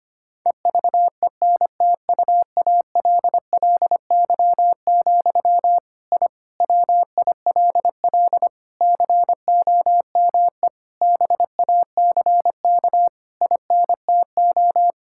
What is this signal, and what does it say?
Morse code, heard as a keyed tone, about 25 wpm, 700 Hz. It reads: EVENTUALLY, I WILL COME BACK INTO
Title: Sentences from Top 1000 Words - Review Part 6 - 25wpm